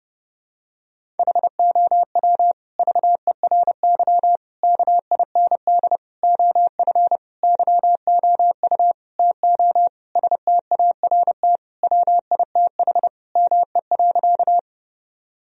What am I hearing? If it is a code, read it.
Morse code, heard as a keyed tone, about 30 wpm, 700 Hz. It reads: HOW VERY KIND OF YOU TO START WITH ME.
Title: Sentences from Top 1000 Words - Review Part 1 - 30wpm